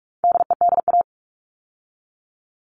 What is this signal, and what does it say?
Morse code, heard as a keyed tone, about 44 wpm, 700 Hz. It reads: DEDA
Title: All Cut Numbers - 4-Digit 45wpm